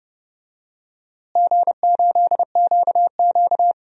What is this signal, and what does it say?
Morse code, heard as a keyed tone, about 30 wpm, 700 Hz. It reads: G8QQ
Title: DXCC-100 Call Signs - 4 Characters - Repeated 2x - 30wpm